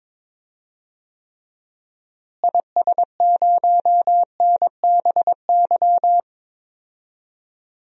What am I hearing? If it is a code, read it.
Morse code, heard as a keyed tone, about 22 wpm, 700 Hz. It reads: IS0NBY
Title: DXCC-100 Call Signs - 6 Characters - Repeated 3x - 22wpm